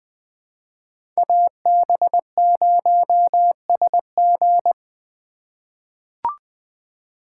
Morse code, heard as a keyed tone, about 20 wpm, 700 Hz. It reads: AB0SG
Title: USA Call Signs - 5 Characters - 2x Character Spacing - 20wpm